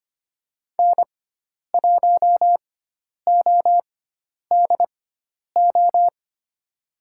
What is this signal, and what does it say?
Morse code, heard as a keyed tone, about 25 wpm, spaced as for 10 wpm, 700 Hz. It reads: N1ODO
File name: Lesson 16g - Letter c - Call Signs 25wpm